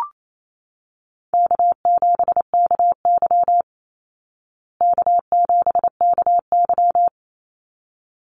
Morse code, heard as a keyed tone, about 28 wpm, 700 Hz. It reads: K7KY K7KY
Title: Call Signs Repeated 3x - 28wpm